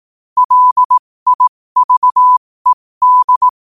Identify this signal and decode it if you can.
Morse code, heard as a keyed tone, about 18 wpm, spaced as for 15 wpm, 1000 Hz. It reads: LIVED
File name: qotdfiles / 15wpm